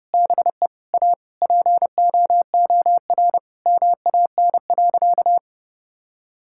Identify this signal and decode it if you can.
Morse code, heard as a keyed tone, about 30 wpm, 700 Hz. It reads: BE A POOR MAN.
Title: Sentences from Top 1000 Words - Review Part 5 - 30wpm